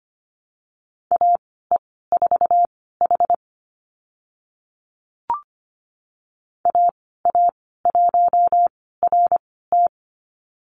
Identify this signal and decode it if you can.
Morse code, heard as a keyed tone, about 25 wpm, spaced as for 16 wpm, 700 Hz. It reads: AE4H AA1RT
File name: Lesson 15g - Number 5 - Call Signs 25wpm